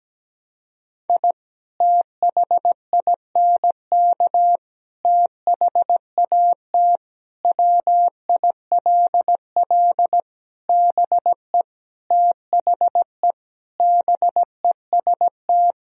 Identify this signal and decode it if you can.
Morse code, heard as a keyed tone, about 17 wpm, 700 Hz. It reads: I THINK THAT WILL BE THE BEST
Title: Sentences from Top 1000 Words - Review Part 2 - 17wpm